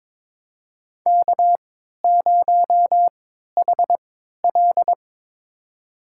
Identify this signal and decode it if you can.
Morse code, heard as a keyed tone, about 22 wpm, spaced as for 12 wpm, 700 Hz. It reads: K0HL
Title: USA Call Signs - 4 Characters - 3x Character Spacing - 22wpm